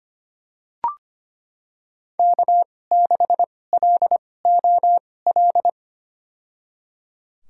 Morse code, heard as a keyed tone, about 25 wpm, spaced as for 18 wpm, 700 Hz. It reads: K6LOL
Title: USA Call Signs - 5 Characters - 2x Character Spacing - 25wpm